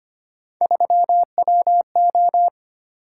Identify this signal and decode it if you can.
Morse code, heard as a keyed tone, about 25 wpm, 700 Hz. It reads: 3WO
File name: Three Letter-Number Combinations - 25wpm